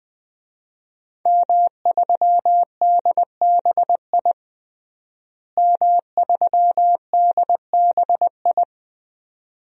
Morse code, heard as a keyed tone, about 21 wpm, 700 Hz. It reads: M3DBI M3DBI
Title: DXCC-100 Call Signs - 5 Characters - Repeated 2x - 20wpm